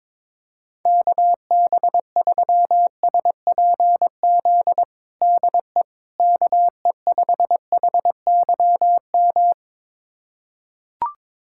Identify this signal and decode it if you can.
Morse code, heard as a keyed tone, about 22 wpm, 700 Hz. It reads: KB3SPZ DE KE5HYM
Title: USA Callsign DE Callsign - 6 Characters - 22wpm